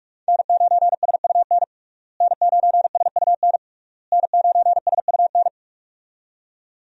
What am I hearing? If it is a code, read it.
Morse code, heard as a keyed tone, about 46 wpm, 700 Hz. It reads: N9SUN N9SUN N9SUN
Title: USA Call Signs - 5 Characters - Repeated 3x - 45wpm